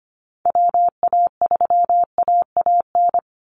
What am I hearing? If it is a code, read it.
Morse code, heard as a keyed tone, about 25 wpm, 700 Hz. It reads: WA3AAN
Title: Call Signs Repeated 3x - 25wpm